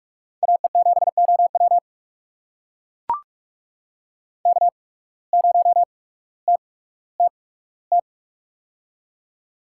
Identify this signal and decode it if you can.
Morse code, heard as a keyed tone, about 41 wpm, spaced as for 19 wpm, 700 Hz. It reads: AE7OW K0TTT
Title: USA Call Signs - 5 Characters - 8x Character Spacing - 45wpm